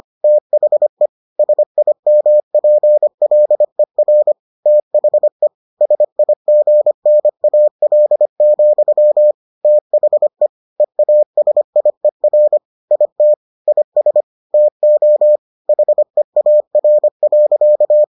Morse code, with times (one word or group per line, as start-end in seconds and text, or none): 0.2-1.1 THE
1.4-4.3 SIMPLER
4.7-5.5 THE
5.8-9.3 SIGNAL,
9.6-10.5 THE
10.8-12.6 EASIER
12.9-13.3 IT
13.7-14.2 IS
14.5-15.4 TO
15.7-18.0 HEAR.